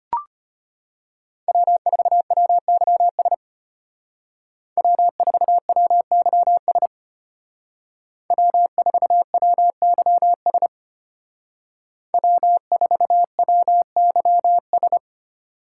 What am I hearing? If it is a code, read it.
Morse code, heard as a keyed tone, about 33 wpm, 700 Hz. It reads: W4WYS W4WYS W4WYS W4WYS